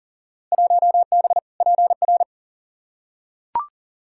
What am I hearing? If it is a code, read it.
Morse code, heard as a keyed tone, about 38 wpm, 700 Hz. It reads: 1B PR
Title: Field Day Exchanges Encore - 40wpm